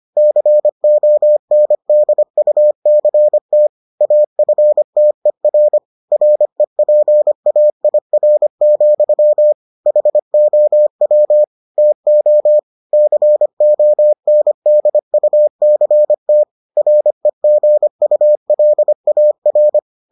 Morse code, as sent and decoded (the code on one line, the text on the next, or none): -.-. --- -. -.. ..- -.-. - / .- ..-. - . .-. / .-. .--. .- .. .-. --..-- / .... --- .-- / - --- / -.-. --- -. -.. ..- -.-. - / .-. --. ..- .-.. .- .-.
CONDUCT AFTER REPAIR, HOW TO CONDUCT REGULAR